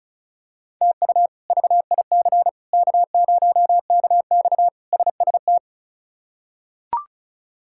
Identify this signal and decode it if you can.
Morse code, heard as a keyed tone, about 36 wpm, 700 Hz. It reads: TU VIC K0KX SST
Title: K1USN SST QSO Practice - 35wpm